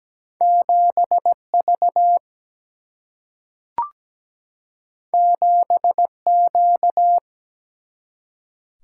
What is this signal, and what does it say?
Morse code, heard as a keyed tone, about 17 wpm, 700 Hz. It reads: 7V 7Q
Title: Single-Letter Single-Number 17wpm